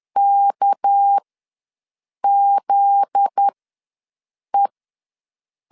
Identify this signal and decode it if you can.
Morse code, heard as a keyed tone, about 10 wpm, spaced as for 6 wpm, 800 Hz. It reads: KZE
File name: LEZIONI / 7 GF9Z / 7 GF9Z ws 10wpm 3919